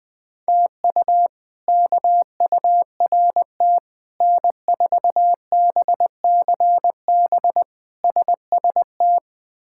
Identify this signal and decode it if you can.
Morse code, heard as a keyed tone, about 20 wpm, 700 Hz. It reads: TU KURT N4BCB SST
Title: K1USN SST QSO Practice - 20wpm